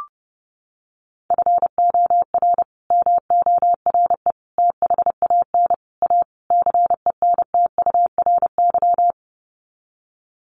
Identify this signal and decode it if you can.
Morse code, heard as a keyed tone, about 30 wpm, 700 Hz. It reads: FOR MORE THAN A CENTURY
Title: Sets of 5 Words Encore2 30wpm